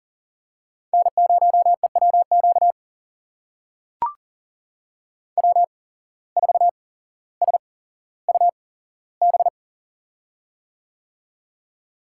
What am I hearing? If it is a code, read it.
Morse code, heard as a keyed tone, about 37 wpm, spaced as for 17 wpm, 700 Hz. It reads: N0EWQ W4SUB
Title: USA Call Signs - 5 Characters - 8x Character Spacing - 40wpm